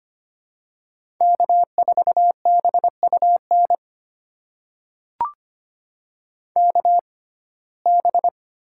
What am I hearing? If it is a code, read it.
Morse code, heard as a keyed tone, about 25 wpm, spaced as for 18 wpm, 700 Hz. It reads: K4BUN KB
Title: USA Call Signs - 6x Character Spacing - 25wpm